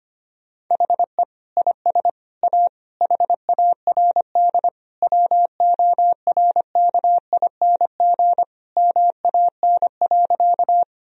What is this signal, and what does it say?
Morse code, heard as a keyed tone, about 25 wpm, 700 Hz. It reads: HE IS A HARD WORKING MAN.
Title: Sentences from Top 1000 Words - Review Part 4 - 25wpm